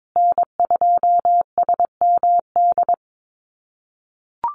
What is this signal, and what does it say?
Morse code, heard as a keyed tone, about 22 wpm, 700 Hz. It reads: N2SMD
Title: USA Call Signs Repeated 2x 22wpm